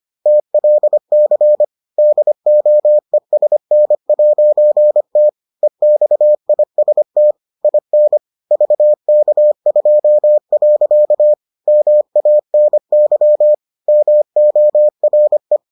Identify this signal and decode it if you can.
Morse code, heard as a keyed tone, about 25 wpm, 600 Hz. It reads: TLC DOESN'T EXIST IN VK2. MANY MORE